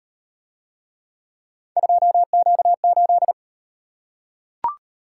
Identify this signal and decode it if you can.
Morse code, heard as a keyed tone, about 37 wpm, 700 Hz. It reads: 2Q8